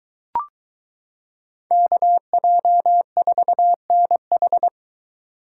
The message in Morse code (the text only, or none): KJ4NH